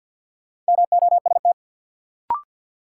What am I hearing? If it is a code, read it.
Morse code, heard as a keyed tone, about 53 wpm, 700 Hz. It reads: MOST
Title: Sentences from Top 200 Words - New Word Drill - 50wpm